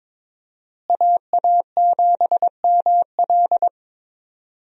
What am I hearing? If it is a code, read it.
Morse code, heard as a keyed tone, about 22 wpm, 700 Hz. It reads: AA7ML
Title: USA Call Signs - 5 Characters - Repeated 3x - 22wpm